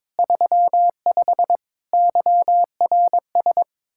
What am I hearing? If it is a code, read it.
Morse code, heard as a keyed tone, about 22 wpm, 700 Hz. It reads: 35 YRS
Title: K1USN SST QSO Practice 22wpm